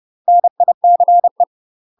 Morse code, heard as a keyed tone, about 30 wpm, 700 Hz. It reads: NICE
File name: UK MASTER QSOs TRAINING / 30WPM / UK MASTER QSOs 0005